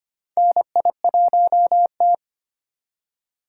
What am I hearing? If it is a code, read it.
Morse code, heard as a keyed tone, about 25 wpm, 700 Hz. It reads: NI1T
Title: USA Call Signs - 4 Characters - 8x Character Spacing - 25wpm